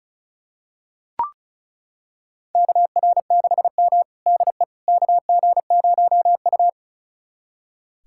Morse code, heard as a keyed tone, about 36 wpm, 700 Hz. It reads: KR6M DE KG0U
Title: USA Callsign DE Callsign - 4 Characters - 35wpm